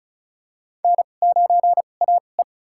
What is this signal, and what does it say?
Morse code, heard as a keyed tone, about 36 wpm, spaced as for 25 wpm, 700 Hz. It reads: N9AE
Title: USA Call Signs - 2x Character Spacing - 35wpm